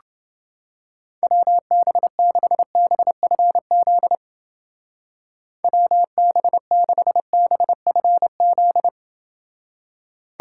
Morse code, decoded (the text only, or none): WB6BFZ WB6BFZ